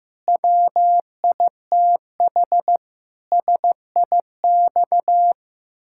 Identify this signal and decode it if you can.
Morse code, heard as a keyed tone, about 15 wpm, 700 Hz. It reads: WITH SIX
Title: Pangram Challenge v2 - 15wpm